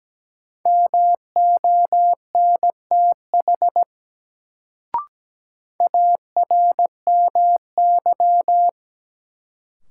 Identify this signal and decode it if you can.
Morse code, heard as a keyed tone, about 17 wpm, 700 Hz. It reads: MONTH ARMY
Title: Sentences from Top 800 Words - New Word Drill - 17wpm